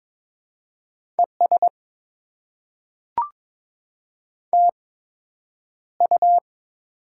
Morse code, heard as a keyed tone, about 22 wpm, spaced as for 9 wpm, 700 Hz. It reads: ES TU
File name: Top 20 QSO Elements - 8x Character Spacing - 22wpm